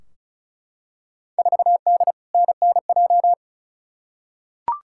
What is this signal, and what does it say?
Morse code, heard as a keyed tone, about 34 wpm, 700 Hz. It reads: 4D NNJ
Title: Field Day Exchanges Encore - 35wpm